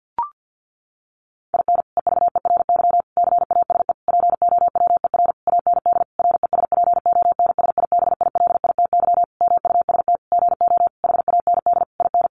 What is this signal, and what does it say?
Morse code, heard as a keyed tone, about 48 wpm, 700 Hz. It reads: IN EVERY CASE POWER AND RESPONSIBILITY MUST GO HAND IN
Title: Sentences from Top 1200 Words 50wpm